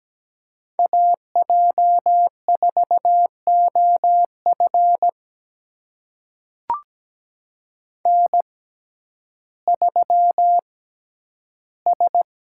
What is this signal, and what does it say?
Morse code, heard as a keyed tone, about 17 wpm, spaced as for 10 wpm, 700 Hz. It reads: AJ4OF N3S
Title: USA Call Signs - 5 Characters - 6x Character Spacing - 17wpm